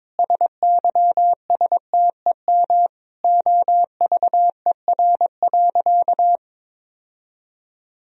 Morse code, heard as a keyed tone, about 22 wpm, 700 Hz. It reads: SYSTEM OVER.
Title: Sentences from Top 1000 Words - Review Part 3 - 22wpm